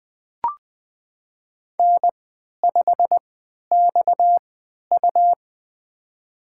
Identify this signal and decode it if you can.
Morse code, heard as a keyed tone, about 20 wpm, spaced as for 11 wpm, 700 Hz. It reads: N5XU
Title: USA Call Signs - 4 Characters - 3x Character Spacing - 20wpm